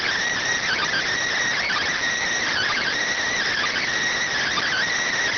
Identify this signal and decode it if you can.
Emergency Alert System (SAME) tones (header bursts), heard as harsh buzzing bursts. It is heard from 0.6 to 4.9 s.